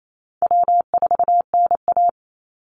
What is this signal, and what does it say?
Morse code, heard as a keyed tone, about 28 wpm, 700 Hz. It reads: W4NA
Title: Easy Call Signs - 28wpm